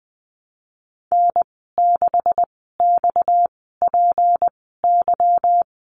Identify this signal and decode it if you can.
Morse code, heard as a keyed tone, about 20 wpm, spaced as for 15 wpm, 700 Hz. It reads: N6XPY
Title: 238 Lesson 38g - Letter q - Call Signs 20wpm